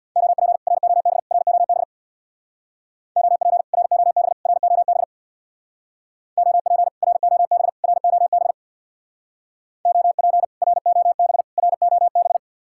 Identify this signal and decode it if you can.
Morse code, heard as a keyed tone, about 66 wpm, 700 Hz. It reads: OP ROB ROB OP ROB ROB OP ROB ROB OP ROB ROB